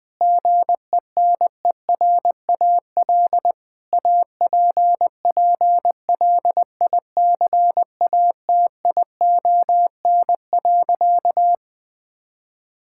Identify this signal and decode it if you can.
Morse code, heard as a keyed tone, about 21 wpm, 700 Hz. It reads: GENERAL APPLICATION.
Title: Sentences from Top 1600 Words 20wpm